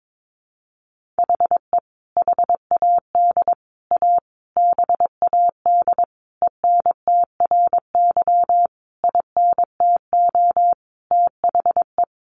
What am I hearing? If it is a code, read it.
Morse code, heard as a keyed tone, about 22 wpm, 700 Hz. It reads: HE HAD A BAD ENTRY INTO THE
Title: Sentences from Top 1500 Words 22wpm